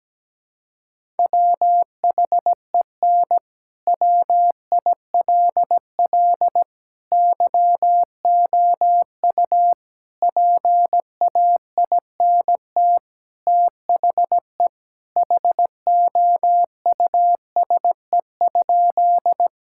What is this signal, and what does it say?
Morse code, heard as a keyed tone, about 17 wpm, 700 Hz. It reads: WHEN WILL YOU PAINT THE HOUSE?
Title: Sentences from Top 1000 Words - Review Part 6 - 17wpm